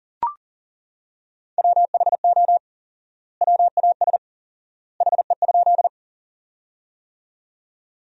Morse code, heard as a keyed tone, about 40 wpm, 700 Hz. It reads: WHO WAS HE?
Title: 4x Word-Spacing - Sentences from Top 100 Words - 40wpm